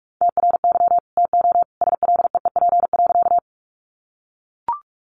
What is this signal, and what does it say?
Morse code, heard as a keyed tone, about 44 wpm, 700 Hz. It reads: TRY TO SLEEP.
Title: Sentences from Top 500 Words 45wpm